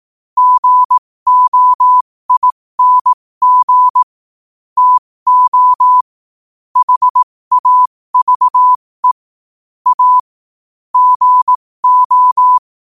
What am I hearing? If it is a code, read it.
Morse code, heard as a keyed tone, about 18 wpm, spaced as for 15 wpm, 1000 Hz. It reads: GOING TO HAVE A GO
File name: qotdfiles / 15wpm